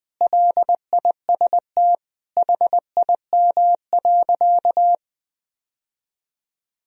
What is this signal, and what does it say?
Morse code, heard as a keyed tone, about 20 wpm, 700 Hz. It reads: LIST HIM.
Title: Sentences from Top 1000 Words - Review Part 2 - 20wpm